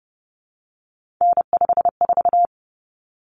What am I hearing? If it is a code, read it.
Morse code, heard as a keyed tone, about 30 wpm, 700 Hz. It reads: N54
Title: Common Cut Numbers - 3-Digit 30wpm